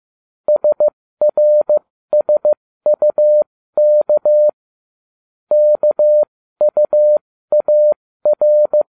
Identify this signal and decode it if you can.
Morse code, heard as a keyed tone, about 15 wpm, spaced as for 13 wpm, 600 Hz. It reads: SRSUK KUAR